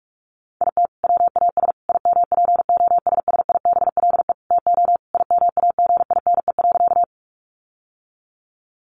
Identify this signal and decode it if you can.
Morse code, heard as a keyed tone, about 44 wpm, 700 Hz. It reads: IT WAS IMPOSSIBLE TO IMAGINE.
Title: Sentences from Top 1400 Words 45wpm